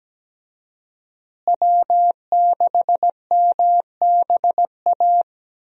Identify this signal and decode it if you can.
Morse code, heard as a keyed tone, about 17 wpm, 700 Hz. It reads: W6MBA
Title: USA Call Signs - 5 Characters - Repeated 2x - 17wpm